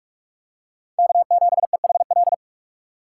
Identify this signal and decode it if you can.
Morse code, heard as a keyed tone, about 45 wpm, 700 Hz. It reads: K7EHL
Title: USA Call Signs - 5 Characters - 2x Character Spacing - 45wpm